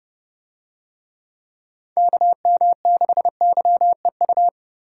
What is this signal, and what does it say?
Morse code, heard as a keyed tone, about 29 wpm, 700 Hz. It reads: KM6YEU